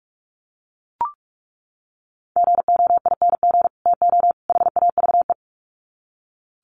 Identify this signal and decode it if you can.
Morse code, heard as a keyed tone, about 44 wpm, 700 Hz. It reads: GOING TO HAVE
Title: Sets of 3 Words 45wpm